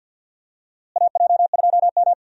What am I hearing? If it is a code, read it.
Morse code, heard as a keyed tone, about 52 wpm, 700 Hz. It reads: AJ2M
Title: USA Call Signs - 4 Characters - Repeated 3x - 50wpm